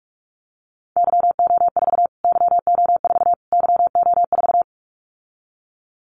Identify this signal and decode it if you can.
Morse code, heard as a keyed tone, about 44 wpm, 700 Hz. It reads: YO4 YO4 YO4
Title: Callsign Prefixes for Top 25 DXCC - Repeated 3x - 45wpm